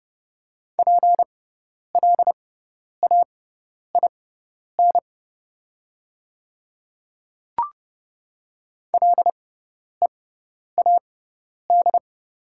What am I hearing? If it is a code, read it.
Morse code, heard as a keyed tone, about 30 wpm, spaced as for 10 wpm, 690 Hz. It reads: PLAIN LEAD
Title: 6x Character Spacing - Top 500 Words - 30wpm